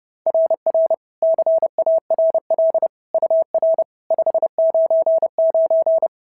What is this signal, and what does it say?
Morse code, heard as a keyed tone, about 26 wpm, 650 Hz. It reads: RR CARL UR 599
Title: SOTA QSO Practice 30wpm